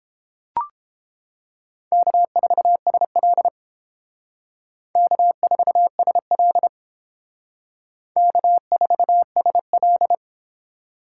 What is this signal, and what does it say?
Morse code, heard as a keyed tone, about 29 wpm, 700 Hz. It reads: K4SL K4SL K4SL